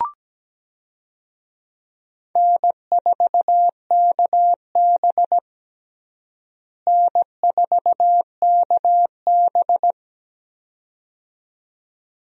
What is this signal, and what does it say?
Morse code, heard as a keyed tone, about 17 wpm, 700 Hz. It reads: N4KB N4KB
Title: USA Call Signs - 4 Characters - Repeated 2x - 17wpm